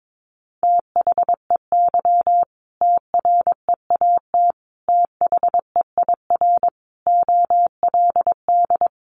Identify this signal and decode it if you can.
Morse code, heard as a keyed tone, about 22 wpm, 700 Hz. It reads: THEY TREAT THEIR OLD
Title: Sentences from Top 1300 Words 22wpm